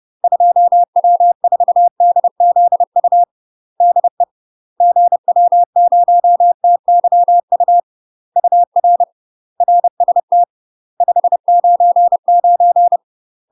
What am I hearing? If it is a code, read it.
Morse code, heard as a keyed tone, about 30 wpm, 700 Hz. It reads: 2W4DZU DE GW0TYU UR RST 599
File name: UK MASTER QSOs TRAINING / 30WPM / UK MASTER QSOs 0045